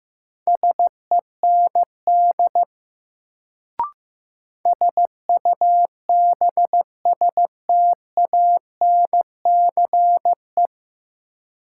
Morse code, heard as a keyed tone, about 15 wpm, 700 Hz. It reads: SEND SUBSTANCE
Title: Sentences from Top 1500 Words - New Word Drill Alt - 15wpm